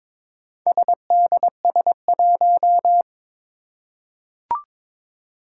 Morse code, heard as a keyed tone, about 22 wpm, 700 Hz. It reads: SDS1
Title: Four Challenging Letter-Number Combinations 22wpm